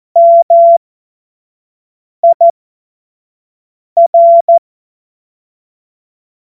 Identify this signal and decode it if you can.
Morse code, heard as a keyed tone, about 14 wpm, spaced as for 5 wpm, 680 Hz. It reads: MIR
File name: Morsekurs / Witze / Witz 06. April 2021, 25 F70